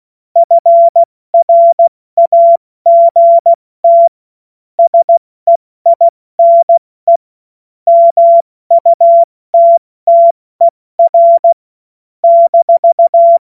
Morse code, heard as a keyed tone, about 16 wpm, 680 Hz. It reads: FRAGT SEINE MUTTER -
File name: Morsekurs / Witze / Witz 21. September 2021, 70 F80